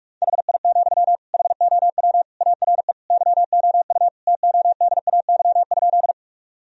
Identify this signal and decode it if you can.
Morse code, heard as a keyed tone, about 46 wpm, 700 Hz. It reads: HI, HOW ARE YOU TODAY?
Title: Sentences from Top 1000 Words - Review Part 3 - 45wpm